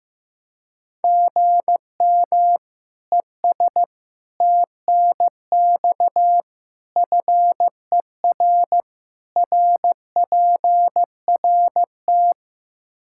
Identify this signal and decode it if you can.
Morse code, heard as a keyed tone, about 15 wpm, 700 Hz. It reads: GM ES TNX FER RPRT
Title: Practice QSO - Exchange 1 - Beginning - 15wpm